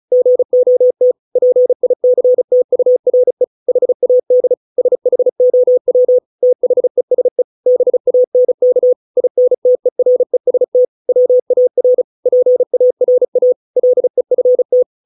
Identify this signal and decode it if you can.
Morse code, heard as a keyed tone, about 35 wpm, 500 Hz. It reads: GOT PICTURE HAD SHOW THESE BANK INTEREST WAR PARA LEFT